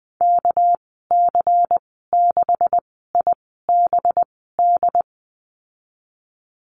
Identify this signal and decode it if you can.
Morse code, heard as a keyed tone, about 20 wpm, spaced as for 15 wpm, 700 Hz. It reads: KC6IBD
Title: 213 Lesson 34g - Number 8 - Call Signs 20wpm